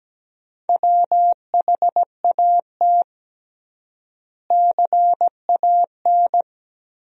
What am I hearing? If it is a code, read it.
Morse code, heard as a keyed tone, about 17 wpm, 700 Hz. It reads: WHAT CAN
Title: 3x Word-Spacing - Sentences from Top 400 Words - 17wpm